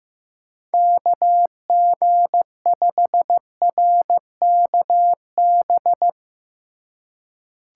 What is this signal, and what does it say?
Morse code, heard as a keyed tone, about 15 wpm, 700 Hz. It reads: KG5RKB
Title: USA Call Signs - 6 Characters - 4x Character Spacing - 15wpm